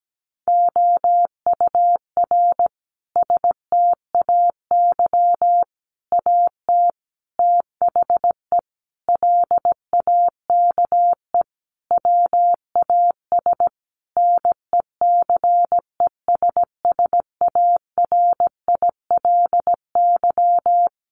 Morse code, heard as a keyed tone, about 17 wpm, 700 Hz. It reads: OUR STAY AT THE LAKE WAS NECESSARILY
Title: Sentences from Top 1400 Words 17wpm